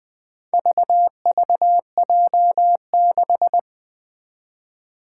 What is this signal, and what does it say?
Morse code, heard as a keyed tone, about 20 wpm, 700 Hz. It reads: VVJ6
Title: Four Challenging Letter-Number Combinations 20wpm